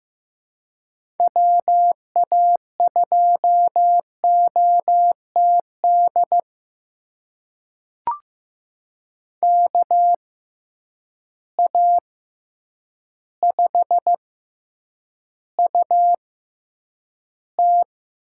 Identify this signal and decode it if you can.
Morse code, heard as a keyed tone, about 15 wpm, spaced as for 8 wpm, 700 Hz. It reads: WA2OTD KA5UT
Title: USA Call Signs - 6 Characters - 6x Character Spacing - 15wpm